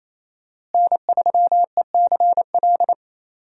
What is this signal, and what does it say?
Morse code, heard as a keyed tone, about 28 wpm, 700 Hz. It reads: N3ECL
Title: USA Call Signs - 5 Characters - Repeated 3x - 28wpm